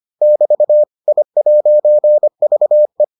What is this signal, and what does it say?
Morse code, heard as a keyed tone, about 25 wpm, 600 Hz. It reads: <BT> I'VE